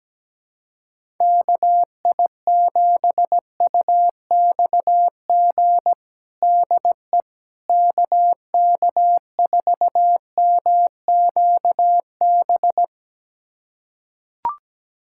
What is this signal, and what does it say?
Morse code, heard as a keyed tone, about 17 wpm, 700 Hz. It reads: KI7UXG DE KK4MQB
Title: USA Callsign DE Callsign - 6 Characters - 17wpm